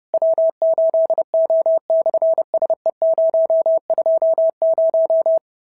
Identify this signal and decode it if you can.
Morse code, heard as a keyed tone, about 30 wpm, 650 Hz. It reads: W8O/SE020
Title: SOTA QSO Practice 30wpm